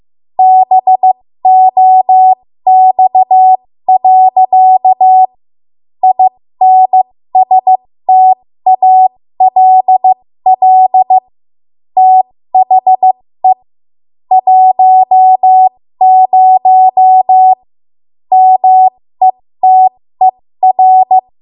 Morse code, heard as a keyed tone, about 15 wpm, spaced as for 13 wpm, 750 Hz. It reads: BOX. INSTALL THE 10 METER
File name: Morse / 13WPM